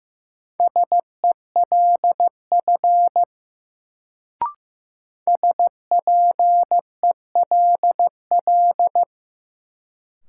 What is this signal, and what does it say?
Morse code, heard as a keyed tone, about 15 wpm, 700 Hz. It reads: SELF SPELL
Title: Sentences from Top 1000 Words - Review Part 1 - New Word Drill - 15wpm